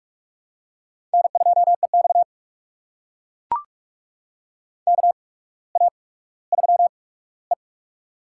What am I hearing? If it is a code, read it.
Morse code, heard as a keyed tone, about 44 wpm, spaced as for 19 wpm, 690 Hz. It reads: N2EX KA3E
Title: USA Call Signs - 4 Characters - 8x Character Spacing - 45wpm